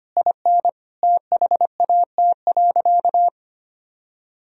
Morse code, heard as a keyed tone, about 25 wpm, 700 Hz. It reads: IN THAT.
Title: Sentences from Top 200 Words and Abbreviations 25wpm